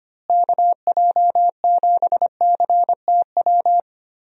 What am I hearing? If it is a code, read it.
Morse code, heard as a keyed tone, about 25 wpm, 700 Hz. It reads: KJ7CTW
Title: USA Call Signs - 6 Characters - Repeated 3x - 25wpm